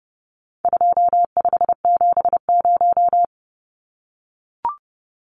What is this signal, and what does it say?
Morse code, heard as a keyed tone, about 30 wpm, 700 Hz. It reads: 2570
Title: Numbers - 4-Digit 30wpm